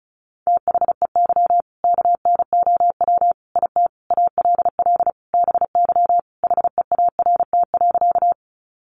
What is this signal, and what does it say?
Morse code, heard as a keyed tone, about 34 wpm, 700 Hz. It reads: THEY KNOW IT ALL BY HEART.
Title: Sentences from Top 1200 Words 35wpm